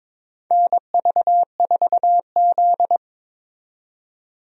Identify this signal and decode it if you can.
Morse code, heard as a keyed tone, about 22 wpm, 700 Hz. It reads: NV4Z
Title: USA Call Signs - 4 Characters - 22wpm